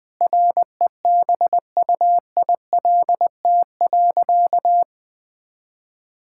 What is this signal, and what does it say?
Morse code, heard as a keyed tone, about 20 wpm, 700 Hz. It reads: REBUILT.
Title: Sentences with 5-Word Phrases - 20wpm